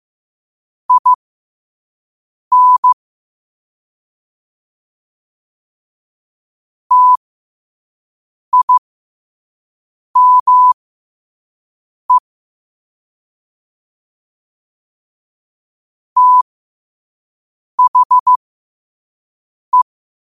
Morse code, heard as a keyed tone, about 15 wpm, spaced as for 5 wpm, 1000 Hz. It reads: IN TIME THE
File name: qotdfiles / 5wpm